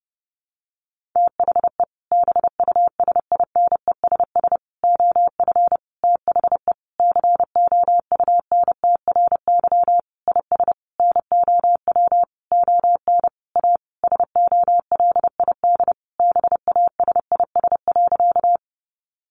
Morse code, heard as a keyed tone, about 30 wpm, 700 Hz. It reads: THE BUSINESS OF THE COUNTRY IS NOW ON A SOLID BASIS.
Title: Sentences from Top 1500 Words 30wpm